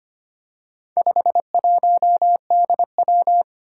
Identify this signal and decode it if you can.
Morse code, heard as a keyed tone, about 25 wpm, 700 Hz. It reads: 51DW
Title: Four Challenging Letter-Number Combinations 25wpm